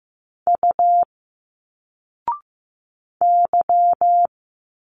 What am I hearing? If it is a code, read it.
Morse code, heard as a keyed tone, about 15 wpm, 700 Hz. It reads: U Y
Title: Single Letters 15wpm